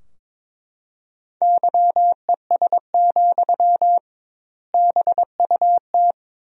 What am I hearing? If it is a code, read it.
Morse code, heard as a keyed tone, about 22 wpm, 700 Hz. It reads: YES, BUT
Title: 2x Word-Spacing - Sentences from Top 500 Words - 22wpm